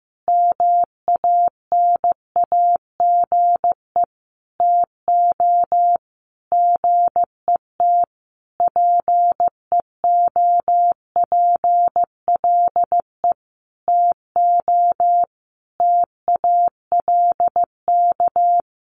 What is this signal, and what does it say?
Morse code, heard as a keyed tone, about 15 wpm, 700 Hz. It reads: MANAGE TO GET PEOPLE TO TALK
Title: Sentences from Top 1400 Words 15wpm